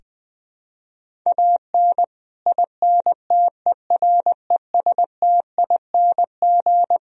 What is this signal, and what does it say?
Morse code, heard as a keyed tone, about 20 wpm, 700 Hz. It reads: AN INTERESTING
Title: Sentences from Top 1000 Words - Review Part 2 - 20wpm